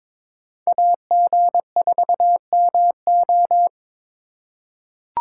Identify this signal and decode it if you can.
Morse code, heard as a keyed tone, about 22 wpm, 700 Hz. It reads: AG4MO
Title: USA Call Signs - 5 Characters - 2x Character Spacing - 22wpm